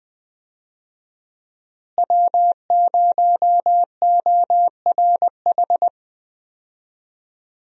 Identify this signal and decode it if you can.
Morse code, heard as a keyed tone, about 20 wpm, 700 Hz. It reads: W0ORH
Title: USA Call Signs - 5 Characters - Repeated 3x - 20wpm